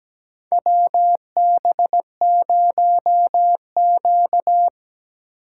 Morse code, heard as a keyed tone, about 17 wpm, 700 Hz. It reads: WB0Q
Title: USA Call Signs - 4 Characters - 6x Character Spacing - 17wpm